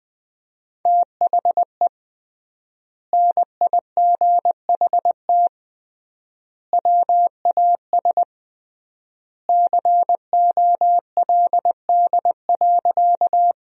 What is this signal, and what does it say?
Morse code, heard as a keyed tone, about 20 wpm, 700 Hz. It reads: THE NIGHT WAS COLD.
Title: 3x Word-Spacing - Sentences from Top 400 Words - 20wpm